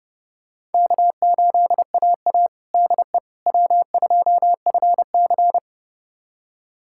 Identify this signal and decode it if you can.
Morse code, heard as a keyed tone, about 30 wpm, 700 Hz. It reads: K8AA DE W2FC
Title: USA Callsign DE Callsign - 4 Characters - 30wpm